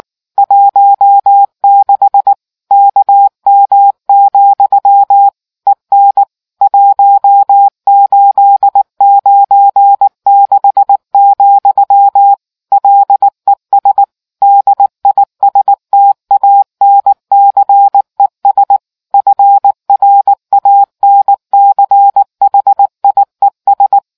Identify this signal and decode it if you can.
Morse code, heard as a keyed tone, about 19 wpm, 800 Hz. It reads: 16 KM, EN 1896, LES DISTANCES FRANCHIES